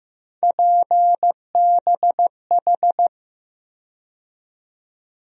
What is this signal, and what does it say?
Morse code, heard as a keyed tone, about 15 wpm, 700 Hz. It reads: PBH
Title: USA Call Sign Postfixes - 3 Letters 15wpm